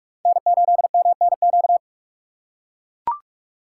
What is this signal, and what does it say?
Morse code, heard as a keyed tone, about 43 wpm, 700 Hz. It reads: N8MNQ
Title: USA Call Signs - 3x Character Spacing - 45wpm